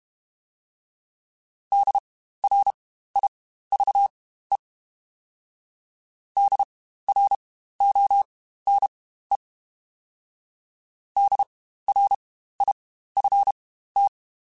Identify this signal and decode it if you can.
Morse code, heard as a keyed tone, about 32 wpm, spaced as for 15 wpm, 800 Hz. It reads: DRIVE DRONE DRIFT
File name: Spaced Alphas / Spaced Alphas mp3s / Ds sp @30